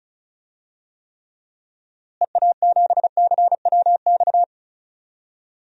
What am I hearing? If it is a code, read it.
Morse code, heard as a keyed tone, about 34 wpm, 700 Hz. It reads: EA7CWX
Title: DXCC-100 Call Signs - 6 Characters - Repeated 2x - 35wpm